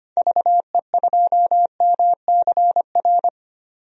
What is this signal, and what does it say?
Morse code, heard as a keyed tone, about 25 wpm, 700 Hz. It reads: VE2MCR
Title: DXCC-100 Call Signs - 6 Characters - 25wpm